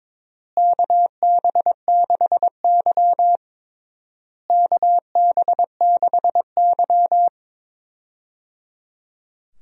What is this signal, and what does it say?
Morse code, heard as a keyed tone, about 22 wpm, 700 Hz. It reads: KB6Y KB6Y
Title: USA Call Signs - 4 Characters - Repeated 3x - 22wpm